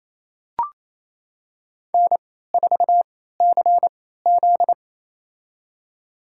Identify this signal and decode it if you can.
Morse code, heard as a keyed tone, about 28 wpm, spaced as for 16 wpm, 700 Hz. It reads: N4CZ
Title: USA Call Signs - 3x Character Spacing - 28wpm